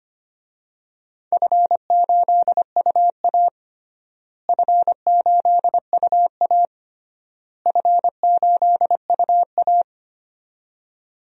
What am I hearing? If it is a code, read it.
Morse code, heard as a keyed tone, about 25 wpm, 700 Hz. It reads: F8UA F8UA F8UA
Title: DXCC-100 Call Signs - 4 Characters - Repeated 3x - 25wpm